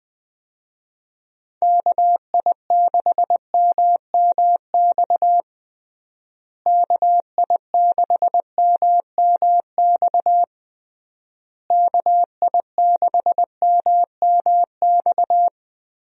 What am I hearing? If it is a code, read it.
Morse code, heard as a keyed tone, about 20 wpm, 700 Hz. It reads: KI6MMX KI6MMX KI6MMX
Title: USA Call Signs - 6 Characters - Repeated 3x - 20wpm